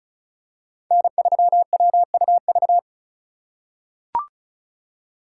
Morse code, heard as a keyed tone, about 34 wpm, 700 Hz. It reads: N3WUV
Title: USA Call Signs - 5 Characters - 4x Character Spacing - 35wpm